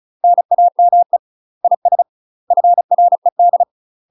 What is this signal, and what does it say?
Morse code, heard as a keyed tone, about 34 wpm, 700 Hz. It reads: NAME IS FRED
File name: UK MASTER QSOs TRAINING / 35WPM / UK MASTER QSOs 0006